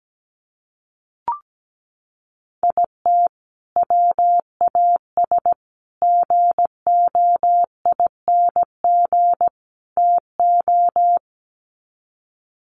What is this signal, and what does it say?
Morse code, heard as a keyed tone, about 17 wpm, 700 Hz. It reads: IT WAS GOING TO
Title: Sets of 4 Words 17wpm